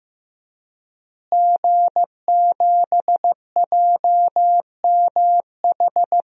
Morse code, heard as a keyed tone, about 15 wpm, 700 Hz. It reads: G7JMH
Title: DXCC-100 Call Signs - 5 Characters - Repeated 3x - 15wpm